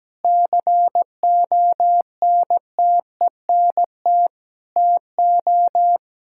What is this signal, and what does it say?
Morse code, heard as a keyed tone, about 17 wpm, 700 Hz. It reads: CONTENT TO
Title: Sentences from Top 1000 Words - Review Part 6 - 17wpm